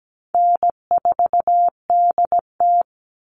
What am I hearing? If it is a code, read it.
Morse code, heard as a keyed tone, about 17 wpm, 700 Hz. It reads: N4DT
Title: Call Signs Repeated 2x - 17wpm